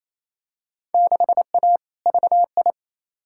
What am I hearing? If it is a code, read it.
Morse code, heard as a keyed tone, about 28 wpm, 700 Hz. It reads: 6A VI
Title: Field Day Exchanges Encore - 28wpm